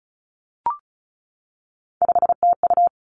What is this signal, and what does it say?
Morse code, heard as a keyed tone, about 36 wpm, 700 Hz. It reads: 5TU
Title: Three Challenging Letter-Number Combinations 35wpm